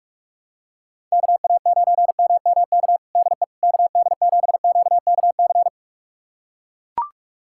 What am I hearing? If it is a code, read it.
Morse code, heard as a keyed tone, about 42 wpm, 700 Hz. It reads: KA9MMK DE KD7QKC
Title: USA Callsign DE Callsign 45wpm